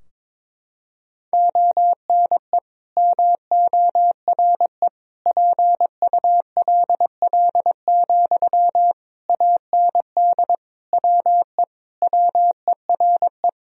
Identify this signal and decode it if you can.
Morse code, heard as a keyed tone, about 22 wpm, 700 Hz. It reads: ONE MORE PULL, AND WE WERE
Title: Sentences from Top 1000 Words - Review Part 2 - 22wpm